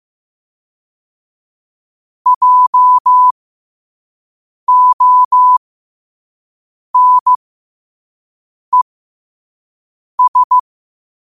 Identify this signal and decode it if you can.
Morse code, heard as a keyed tone, about 15 wpm, spaced as for 5 wpm, 1000 Hz. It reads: JONES
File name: qotdfiles / 5wpm